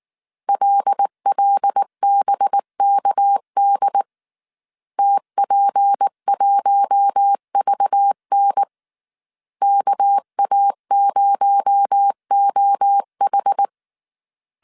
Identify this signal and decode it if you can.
Morse code, heard as a keyed tone, about 19 wpm, 800 Hz. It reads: LLBKD TP1VN KA0OH